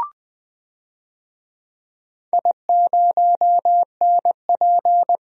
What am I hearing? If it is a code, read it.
Morse code, heard as a keyed tone, about 21 wpm, 700 Hz. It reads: I0NP
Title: DXCC-100 Call Signs - 4 Characters - 20wpm